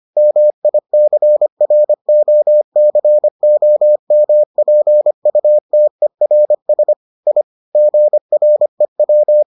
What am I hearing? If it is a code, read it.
Morse code, heard as a keyed tone, about 25 wpm, 600 Hz. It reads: MICROCOMPUTERS I GREW